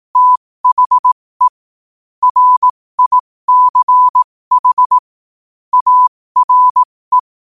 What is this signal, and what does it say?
Morse code, heard as a keyed tone, about 18 wpm, spaced as for 15 wpm, 1000 Hz. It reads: THE RICH ARE
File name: qotdfiles / 15wpm